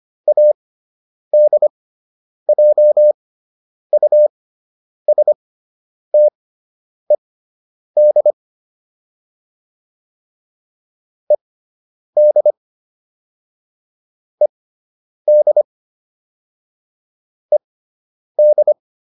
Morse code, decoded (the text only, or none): ADJUSTED ED ED ED